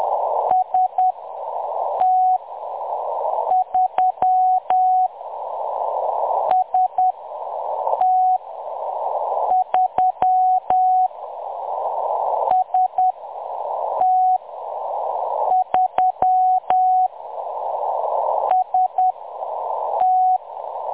Morse code, heard as a keyed tone, about 10 wpm, spaced as for 6 wpm, 740 Hz. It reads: ST3 ST3 ST3 ST
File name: M23-ST3- -0858utc-10184khz